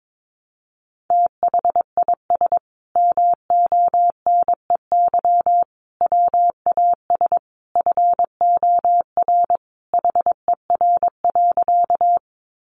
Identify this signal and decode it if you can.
Morse code, heard as a keyed tone, about 22 wpm, 700 Hz. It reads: THIS MONEY WAS FOR HER.
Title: Sentences from Top 400 Words 22wpm